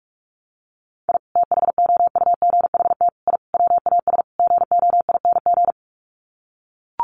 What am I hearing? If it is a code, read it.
Morse code, heard as a keyed tone, about 44 wpm, 700 Hz. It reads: I THOUGHT I WAS GOING
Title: Sets of 5 Words Encore2 45wpm